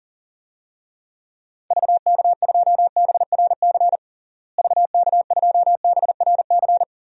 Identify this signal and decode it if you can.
Morse code, heard as a keyed tone, about 40 wpm, 700 Hz. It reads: VK2BRC VK2BRC
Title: DXCC-100 Call Signs - 6 Characters - Repeated 3x - 40wpm